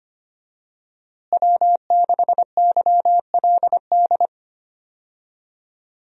Morse code, heard as a keyed tone, about 25 wpm, 700 Hz. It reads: W6YLD
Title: USA Call Signs - 5 Characters - Repeated 3x - 25wpm